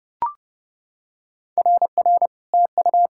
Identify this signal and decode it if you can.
Morse code, heard as a keyed tone, about 30 wpm, 700 Hz. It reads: RR TU
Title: SOTA QSO Practice 30wpm